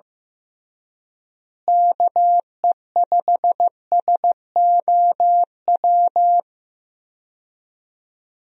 Morse code, heard as a keyed tone, about 15 wpm, 700 Hz. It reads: KE5SOW
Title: USA Call Signs - 6 Characters - Repeated 3x - 15wpm